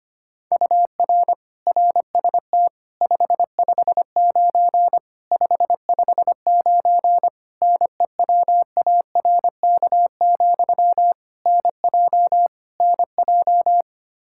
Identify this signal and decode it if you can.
Morse code, heard as a keyed tone, about 25 wpm, 700 Hz. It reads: UR RST 559 559 NEWARK, NJ NJ
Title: Practice QSO - Exchange 1 - Beginning - 25wpm